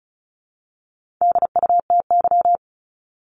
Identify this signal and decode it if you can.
Morse code, heard as a keyed tone, about 34 wpm, 700 Hz. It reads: DUTY
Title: Top 1251-1500 Words 35wpm